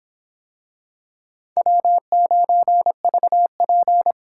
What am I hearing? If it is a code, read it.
Morse code, heard as a keyed tone, about 26 wpm, 700 Hz. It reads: W9VP